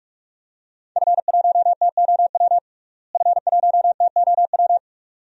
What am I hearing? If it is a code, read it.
Morse code, heard as a keyed tone, about 44 wpm, 700 Hz. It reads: F1TOW F1TOW
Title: DXCC-100 Call Signs - 5 Characters - Repeated 2x - 45wpm